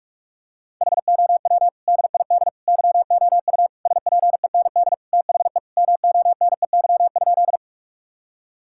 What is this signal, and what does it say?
Morse code, heard as a keyed tone, about 47 wpm, 700 Hz. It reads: HOW DID YOU SPEND THE MONEY?
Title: Sentences from Top 1000 Words - Review Part 5 - 45wpm